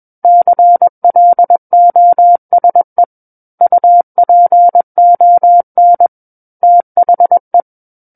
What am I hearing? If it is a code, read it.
Morse code, heard as a keyed tone, about 21 wpm, 700 Hz. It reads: CLOSE UPON THE
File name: Book One / 12-Chapter 12 - 21 WPM